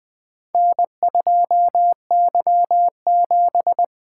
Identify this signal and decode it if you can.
Morse code, heard as a keyed tone, about 20 wpm, 700 Hz. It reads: N2Y7
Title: Four Letter-Number Combinations - 20wpm